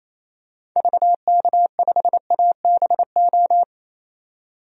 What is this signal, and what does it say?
Morse code, heard as a keyed tone, about 27 wpm, 700 Hz. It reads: VK5ABO
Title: DXCC-100 Call Signs - 6 Characters - 28wpm